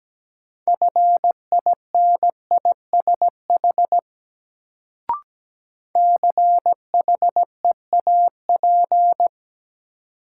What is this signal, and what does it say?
Morse code, heard as a keyed tone, about 17 wpm, 700 Hz. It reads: FINISH CHEAP
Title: Sentences from Top 1200 Words - New Word Drill Alt - 17wpm